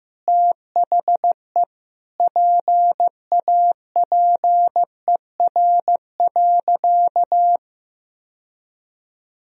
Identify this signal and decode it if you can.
Morse code, heard as a keyed tone, about 15 wpm, 700 Hz. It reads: THE PAPER.
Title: Sentences from Top 1000 Words - Review Part 1 - 15wpm